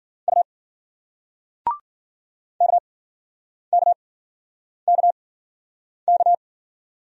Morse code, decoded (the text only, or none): U KKKK